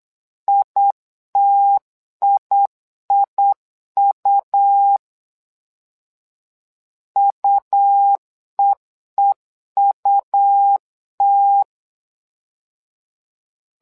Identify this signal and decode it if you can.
Morse code, heard as a keyed tone, about 8 wpm, 800 Hz. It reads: ITIIU UEEUT
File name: LEZIONI / 2 UHDN / 2 U 8wpm 2028